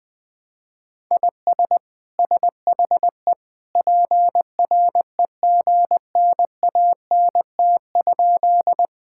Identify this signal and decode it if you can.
Morse code, heard as a keyed tone, about 20 wpm, 700 Hz. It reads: IS SHE PREGNANT?
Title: Sentences from Top 1000 Words - Review Part 6 - 20wpm